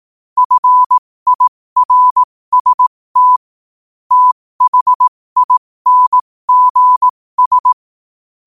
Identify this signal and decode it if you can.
Morse code, heard as a keyed tone, about 18 wpm, spaced as for 15 wpm, 1000 Hz. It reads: FIRST THINGS
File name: qotdfiles / 15wpm